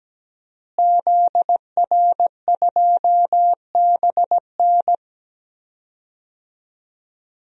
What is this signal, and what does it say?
Morse code, heard as a keyed tone, about 17 wpm, 700 Hz. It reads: ZR2BN
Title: DXCC-100 Call Signs - 5 Characters - Repeated 3x - 17wpm